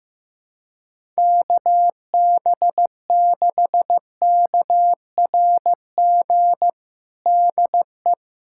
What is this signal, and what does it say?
Morse code, heard as a keyed tone, about 15 wpm, 700 Hz. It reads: KB6KRG DE
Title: USA Callsign DE Callsign - 6 Characters - 15wpm